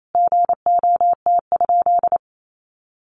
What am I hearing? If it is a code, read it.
Morse code, heard as a keyed tone, about 28 wpm, 700 Hz. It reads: GOT?
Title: Sentences from Top 300 Words 28wpm